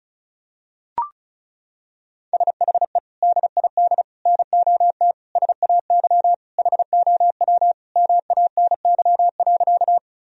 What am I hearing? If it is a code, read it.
Morse code, heard as a keyed tone, about 36 wpm, 700 Hz. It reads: SHE DID NOT SAY HOW MANY.
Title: Sentences from Top 1000 Words - Review Part 1 - 35wpm